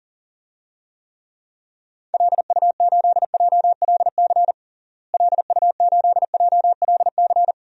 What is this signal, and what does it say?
Morse code, heard as a keyed tone, about 40 wpm, 700 Hz. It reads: LU8JLC LU8JLC
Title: DXCC-100 Call Signs - 6 Characters - Repeated 2x - 40wpm